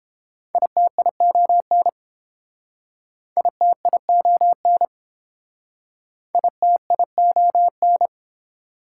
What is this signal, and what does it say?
Morse code, heard as a keyed tone, about 32 wpm, 690 Hz. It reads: ITION ITION ITION